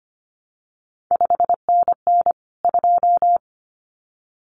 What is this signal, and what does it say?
Morse code, heard as a keyed tone, about 25 wpm, 700 Hz. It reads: 5NN 2
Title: CQ WW DX Contest Exchange - 25wpm